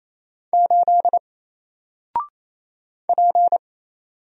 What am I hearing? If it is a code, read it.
Morse code, heard as a keyed tone, about 28 wpm, 700 Hz. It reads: 8 P
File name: Single-Letter-Number-28wpm